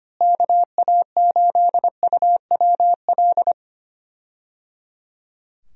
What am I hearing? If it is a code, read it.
Morse code, heard as a keyed tone, about 25 wpm, 700 Hz. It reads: KA8UWL
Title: USA Call Signs - 6 Characters - Repeated 3x - 25wpm